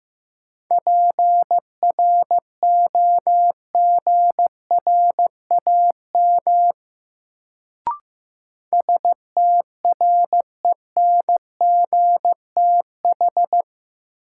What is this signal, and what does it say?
Morse code, heard as a keyed tone, about 16 wpm, 700 Hz. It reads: PROGRAM STRENGTH
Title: Sentences from Top 1000 Words - Review Part 4 - New Word Drill - 15wpm